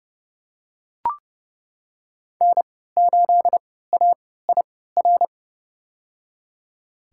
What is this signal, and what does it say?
Morse code, heard as a keyed tone, about 30 wpm, spaced as for 17 wpm, 700 Hz. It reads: N8AIR
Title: USA Call Signs - 5 Characters - 3x Character Spacing - 30wpm